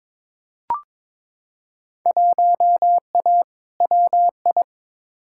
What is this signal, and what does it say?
Morse code, heard as a keyed tone, about 22 wpm, 700 Hz. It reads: 1A WI
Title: Field Day Exchanges - 22wpm